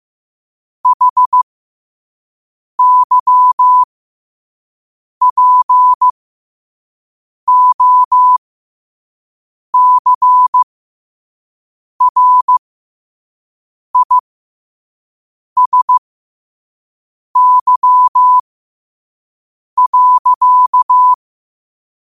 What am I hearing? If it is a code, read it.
Morse code, heard as a keyed tone, about 15 wpm, spaced as for 5 wpm, 1000 Hz. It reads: HYPOCRISY.
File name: qotdfiles / 5wpm